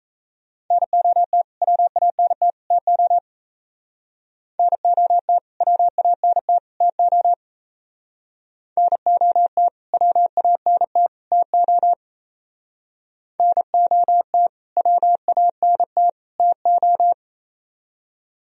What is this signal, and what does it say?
Morse code, heard as a keyed tone, about 33 wpm, 700 Hz. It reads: NOT WANT TO NOT WANT TO NOT WANT TO NOT WANT TO
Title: Speed-Racing - Sets of 3 Words 28wpm